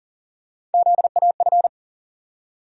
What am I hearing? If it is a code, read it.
Morse code, heard as a keyed tone, about 40 wpm, 700 Hz. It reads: ZAF
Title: USA Call Sign Postfixes - 3 Letters 40wpm